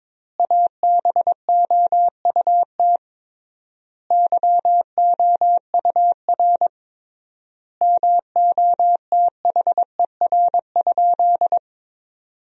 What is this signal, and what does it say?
Morse code, heard as a keyed tone, about 22 wpm, 700 Hz. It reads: ABOUT YOUR MOTHER?
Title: 3x Word-Spacing - Sentences from Top 200 Words - 22wpm